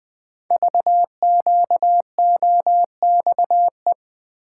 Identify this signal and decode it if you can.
Morse code, heard as a keyed tone, about 20 wpm, 700 Hz. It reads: VQOXE
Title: Code Groups - 5 Letters - 20wpm